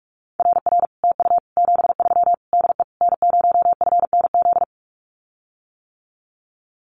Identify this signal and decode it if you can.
Morse code, heard as a keyed tone, about 44 wpm, 700 Hz. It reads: RR TU 73 DE N0FNZ
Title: Common QSO Phrases Encore 45wpm